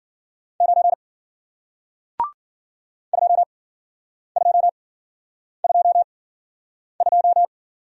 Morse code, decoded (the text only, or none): 9 2222